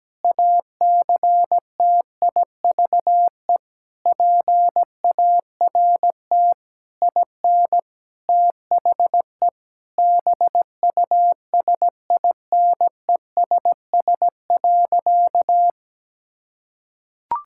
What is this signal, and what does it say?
Morse code, heard as a keyed tone, about 17 wpm, 700 Hz. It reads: ACTIVE PART IN THE BUSINESS.
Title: Sentences from Top 1000 Words - Review Part 6 - 17wpm